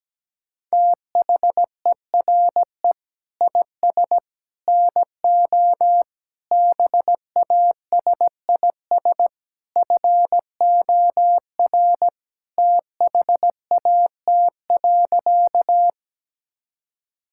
Morse code, heard as a keyed tone, about 17 wpm, 700 Hz. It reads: THERE IS NO BASIS FOR THAT.
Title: Sentences from Top 1000 Words - Review Part 5 - 17wpm